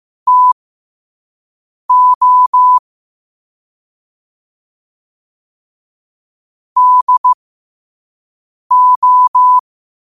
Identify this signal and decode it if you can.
Morse code, heard as a keyed tone, about 15 wpm, spaced as for 5 wpm, 1000 Hz. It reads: TO DO
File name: qotdfiles / 5wpm